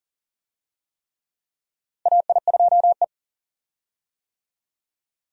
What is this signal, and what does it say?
Morse code, heard as a keyed tone, about 39 wpm, 700 Hz. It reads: AI2E
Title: USA Call Signs - 4 Characters - Repeated 3x - 40wpm